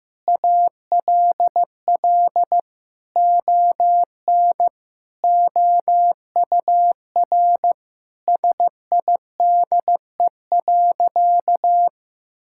Morse code, heard as a keyed tone, about 15 wpm, 700 Hz. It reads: ALL ON OUR SIDE.
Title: Sentences from Top 1000 Words - Review Part 6 - 15wpm